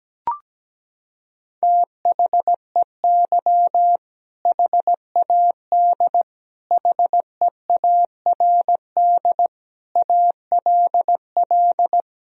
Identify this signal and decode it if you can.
Morse code, heard as a keyed tone, about 17 wpm, 700 Hz. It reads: THEY HAD HEARD ALL
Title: Sentences from Top 400 Words and Abbreviations 17wpm